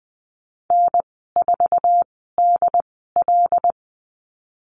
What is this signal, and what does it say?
Morse code, heard as a keyed tone, about 20 wpm, spaced as for 15 wpm, 700 Hz. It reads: N4DL
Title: 076 Lesson 13g - Letter L - Call Signs 20wpm